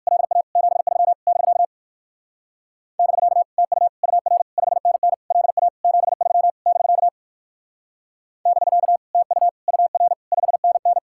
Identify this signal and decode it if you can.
Morse code, heard as a keyed tone, about 55 wpm, 700 Hz. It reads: LA 73 <BK> <BK> TU UR 5NN LA 73 <BK> <BK> TU UR 5NN
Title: Speed-Racing Common QSO Phrases - 45wpm